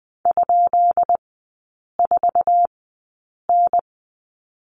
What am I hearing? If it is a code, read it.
Morse code, heard as a keyed tone, about 20 wpm, spaced as for 8 wpm, 700 Hz. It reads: ?4N
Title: 138 Lesson 23e - Letter f - Three Characters 20wpm